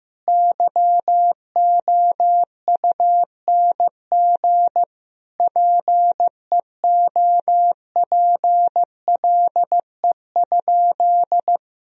Morse code, heard as a keyed tone, about 16 wpm, 700 Hz. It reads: YOUNG PEOPLE?
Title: Sentences from Top 1000 Words - Review Part 1 - 15wpm